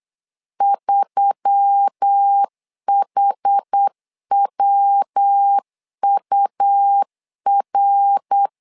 Morse code, heard as a keyed tone, about 8 wpm, 800 Hz. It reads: 3HWUR
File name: LEZIONI / 4 Q1S3 8wpm 3037